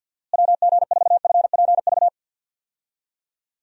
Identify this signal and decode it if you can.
Morse code, heard as a keyed tone, about 52 wpm, 700 Hz. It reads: WG4FPV
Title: USA Call Signs - 6 Characters - Repeated 2x - 50wpm